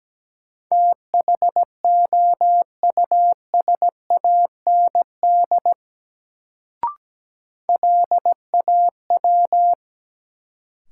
Morse code, heard as a keyed tone, about 17 wpm, 700 Hz. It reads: THOUSAND LAW
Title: Sentences from Top 600 Words - New Word Drill - 17wpm